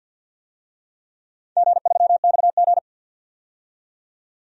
Morse code, heard as a keyed tone, about 51 wpm, 700 Hz. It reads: G3XG
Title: DXCC-100 Call Signs - 4 Characters - 50wpm